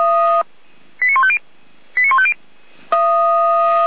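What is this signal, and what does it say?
Five-tone selective call (ZVEI), heard as quick runs of five tones, tones of about 70 ms. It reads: 89150 89150
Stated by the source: ZVEI five-tone selective call received with an SDR.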